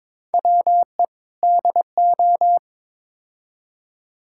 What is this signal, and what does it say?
Morse code, heard as a keyed tone, about 21 wpm, 700 Hz. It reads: WE DO
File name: practice / 22wpm / Build A Phrase - 5-Word Phrases - 22wpm